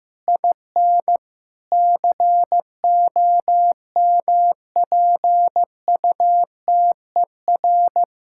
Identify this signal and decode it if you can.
Morse code, heard as a keyed tone, about 15 wpm, 700 Hz. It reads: IN COMPUTER
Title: Sentences from Top 1000 Words - Review Part 6 - 15wpm